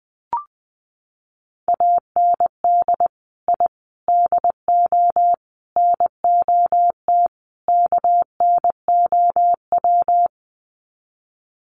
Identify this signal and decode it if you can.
Morse code, heard as a keyed tone, about 20 wpm, 700 Hz. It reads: AND I DO NOT KNOW
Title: Sets of 5 Words 20wpm